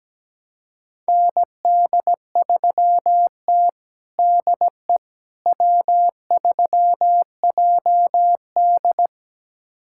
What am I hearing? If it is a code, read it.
Morse code, heard as a keyed tone, about 17 wpm, 700 Hz. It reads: ND3T DE W3JD
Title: USA Callsign DE Callsign - 4 Characters - 17wpm